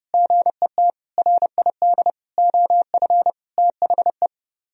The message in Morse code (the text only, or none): GET RID OF THE